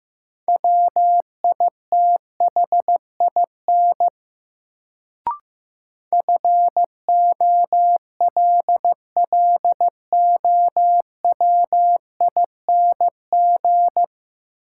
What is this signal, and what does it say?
Morse code, heard as a keyed tone, about 15 wpm, 700 Hz. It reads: WITHIN FOLLOWING
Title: Sentences from Top 700 Words - New Word Drill - 15wpm